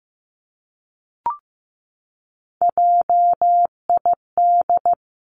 Morse code, heard as a keyed tone, about 15 wpm, 700 Hz. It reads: JID
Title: Three Challenging Letter-Number Combinations 15wpm